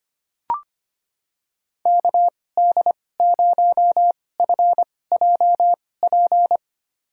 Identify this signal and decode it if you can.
Morse code, heard as a keyed tone, about 25 wpm, spaced as for 18 wpm, 700 Hz. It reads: KD0FJP
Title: USA Call Signs - 6 Characters - 2x Character Spacing - 25wpm